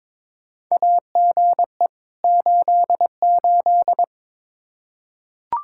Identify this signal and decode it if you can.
Morse code, heard as a keyed tone, about 22 wpm, 700 Hz. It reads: AGE 88
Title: Practice QSO - Exchange 3 - More Info - 22wpm